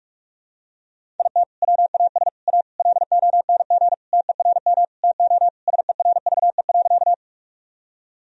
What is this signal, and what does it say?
Morse code, heard as a keyed tone, about 42 wpm, 700 Hz. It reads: IT WAS A LONG TERM TO SERVE.
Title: Sentences from Top 1000 Words - Review Part 4 - 45wpm